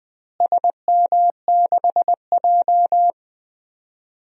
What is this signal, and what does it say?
Morse code, heard as a keyed tone, about 20 wpm, 700 Hz. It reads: SM6J
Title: DXCC-100 Call Signs - 4 Characters - Repeated 3x - 20wpm